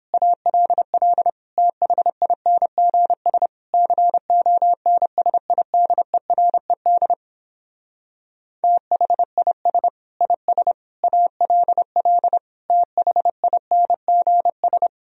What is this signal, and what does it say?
Morse code, heard as a keyed tone, about 31 wpm, 700 Hz. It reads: ALL THINGS CONSIDERED THIS IS ALL THINGS